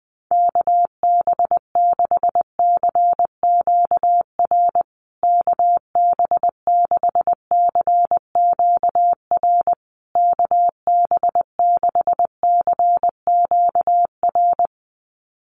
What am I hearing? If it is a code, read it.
Morse code, heard as a keyed tone, about 20 wpm, 700 Hz. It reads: KB6CQR KB6CQR KB6CQR
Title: USA Call Signs Repeated 3x 20wpm